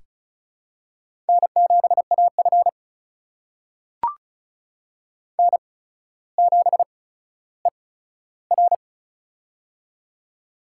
Morse code, heard as a keyed tone, about 36 wpm, spaced as for 15 wpm, 700 Hz. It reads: N7AF N7ER
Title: USA Call Signs - 4 Characters - 8x Character Spacing - 35wpm